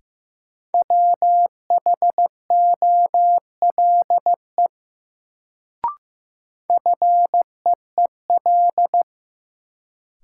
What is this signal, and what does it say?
Morse code, heard as a keyed tone, about 15 wpm, 700 Hz. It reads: WHOLE FEEL
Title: Sentences from Top 400 Words - New Word Drill - 15wpm